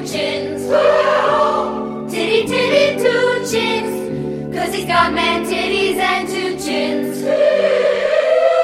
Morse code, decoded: none